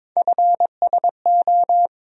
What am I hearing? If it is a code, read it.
Morse code, heard as a keyed tone, about 22 wpm, 700 Hz. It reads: FSO
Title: USA Call Sign Postfixes - 3 Letters 22wpm